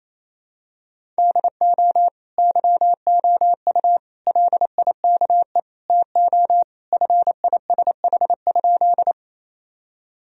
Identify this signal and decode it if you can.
Morse code, heard as a keyed tone, about 27 wpm, 700 Hz. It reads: DO YOU LIKE TO FISH?
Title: Sentences from Top 1000 Words - Review Part 1 - 28wpm